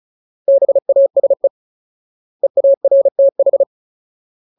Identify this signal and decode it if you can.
Morse code, heard as a keyed tone, about 34 wpm, 550 Hz. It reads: BASE EARTH
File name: Ultimate Word List / 35WPM Ultimate